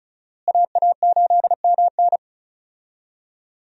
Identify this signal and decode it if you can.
Morse code, heard as a keyed tone, about 34 wpm, 700 Hz. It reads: AA8MN
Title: USA Call Signs - 5 Characters - Repeated 3x - 35wpm